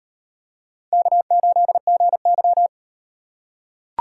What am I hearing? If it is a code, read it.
Morse code, heard as a keyed tone, about 38 wpm, 700 Hz. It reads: K8GY